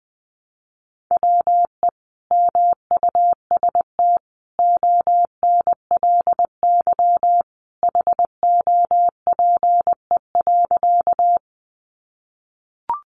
Morse code, heard as a keyed tone, about 20 wpm, 700 Hz. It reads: WE MUST ONLY HOPE.
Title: Sentences from Top 1000 Words 20wpm